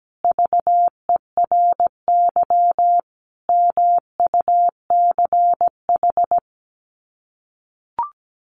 Morse code, heard as a keyed tone, about 17 wpm, 700 Hz. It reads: VERY MUCH
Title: Sets of 5 Words 17wpm